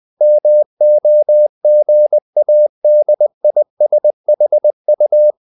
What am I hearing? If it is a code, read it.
Morse code, heard as a keyed tone, about 20 wpm, 600 Hz. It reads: MOGADISHU
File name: CW / Practice / NationalCapitals20